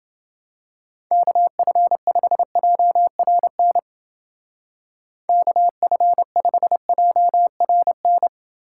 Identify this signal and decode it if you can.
Morse code, heard as a keyed tone, about 28 wpm, 700 Hz. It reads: KF5JRN KF5JRN